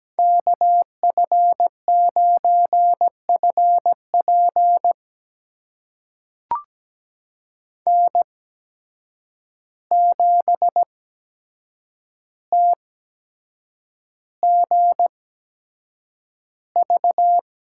Morse code, heard as a keyed tone, about 17 wpm, spaced as for 7 wpm, 700 Hz. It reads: KF9FP N7TGV
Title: USA Call Signs - 5 Characters - 8x Character Spacing - 17wpm